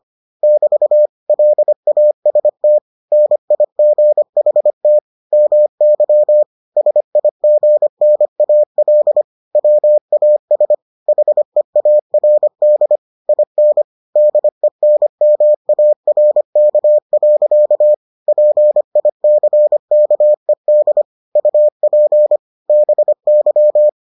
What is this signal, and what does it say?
Morse code, heard as a keyed tone, about 25 wpm, 600 Hz. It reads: <BT> LAST NIGHT MY SIGNAL WAS HEARD IN DENMARK. PICKED UP BY